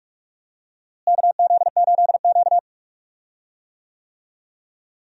Morse code, heard as a keyed tone, about 46 wpm, 700 Hz. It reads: KZ8Q
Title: USA Call Signs - 4 Characters - Repeated 2x - 45wpm